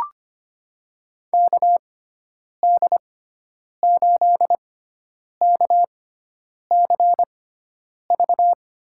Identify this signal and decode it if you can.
Morse code, heard as a keyed tone, about 25 wpm, spaced as for 9 wpm, 700 Hz. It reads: KD8KCV
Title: USA Call Signs - 6 Characters - 6x Character Spacing - 25wpm